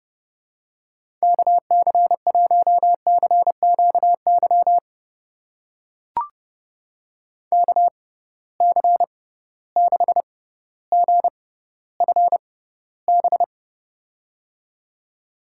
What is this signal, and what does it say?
Morse code, heard as a keyed tone, about 30 wpm, spaced as for 15 wpm, 700 Hz. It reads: KC1CQY KC6GFB
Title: USA Call Signs - 6x Character Spacing - 30wpm